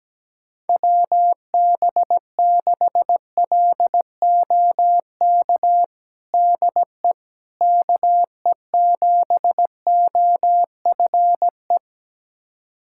Morse code, heard as a keyed tone, about 17 wpm, 700 Hz. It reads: WB6LOK DE KE7OFE
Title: USA Callsign DE Callsign - 6 Characters - 17wpm